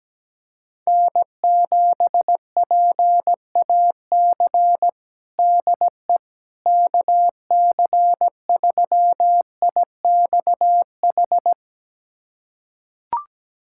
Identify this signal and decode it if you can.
Morse code, heard as a keyed tone, about 17 wpm, 700 Hz. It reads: N7PAC DE KC3IXH
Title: USA Callsign DE Callsign 17wpm